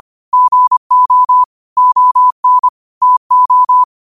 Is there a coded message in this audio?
Yes: Morse code from 0.3 to 3.8 s, heard as a keyed tone, about 25 wpm, 1000 Hz. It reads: GO ON TO